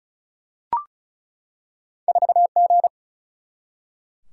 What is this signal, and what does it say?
Morse code, heard as a keyed tone, about 36 wpm, 700 Hz. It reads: 4G
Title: Single-Letter Single-Number 35wpm